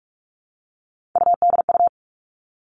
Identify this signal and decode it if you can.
Morse code, heard as a keyed tone, about 46 wpm, 700 Hz. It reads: UDU
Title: Three Challenging Letter-Number Combinations 45wpm